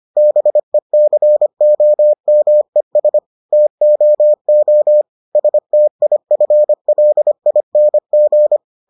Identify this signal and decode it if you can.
Morse code, heard as a keyed tone, about 25 wpm, 600 Hz. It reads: BECOMES TOO STIFLING